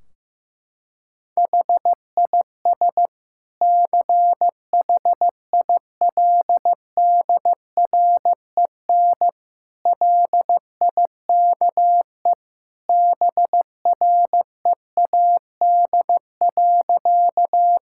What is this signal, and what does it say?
Morse code, heard as a keyed tone, about 15 wpm, 700 Hz. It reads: HIS CHILDREN LIKE BREAD.
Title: Sentences from Top 1600 Words 15wpm